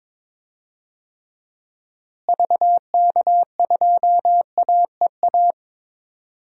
Morse code, heard as a keyed tone, about 22 wpm, 700 Hz. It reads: VK2AEA
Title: DXCC-100 Call Signs - 6 Characters - Repeated 2x - 22wpm